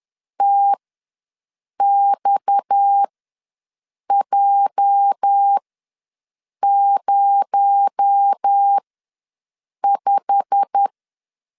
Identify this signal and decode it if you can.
Morse code, heard as a keyed tone, about 10 wpm, spaced as for 6 wpm, 800 Hz. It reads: TXJ05
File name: LEZIONI / 10 SIGNS / 10C DOT ws 10wpm 4213